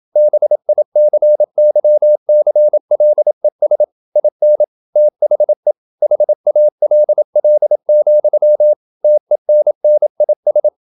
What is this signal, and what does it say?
Morse code, heard as a keyed tone, about 27 wpm, 600 Hz. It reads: BICYCLES IN THE HALL, TENNIS
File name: Never 0027wpm